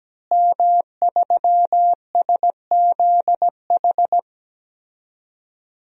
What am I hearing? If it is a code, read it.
Morse code, heard as a keyed tone, about 17 wpm, 700 Hz. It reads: M3SZH
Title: DXCC-100 Call Signs - 5 Characters - Repeated 3x - 17wpm